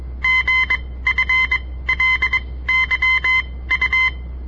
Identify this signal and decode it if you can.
Morse code, heard as a keyed tone, about 21 wpm, spaced as for 16 wpm, 2050 Hz. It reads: GFLYU
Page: On the air Morse code practice from the Southern Montana Amatuer Radio Association; all code, no read back